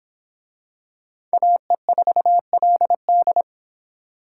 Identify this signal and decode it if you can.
Morse code, heard as a keyed tone, about 23 wpm, 700 Hz. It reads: AE4LD